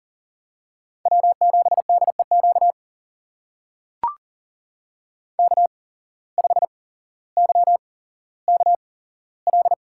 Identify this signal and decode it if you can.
Morse code, heard as a keyed tone, about 40 wpm, spaced as for 17 wpm, 700 Hz. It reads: W7DEQ K5YKL
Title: USA Call Signs - 5 Characters - 8x Character Spacing - 40wpm